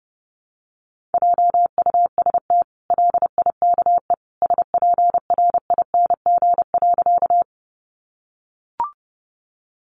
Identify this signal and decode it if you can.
Morse code, heard as a keyed tone, about 30 wpm, 700 Hz. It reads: JUST LIKE SPRING.
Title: Sentences from Top 1200 Words 30wpm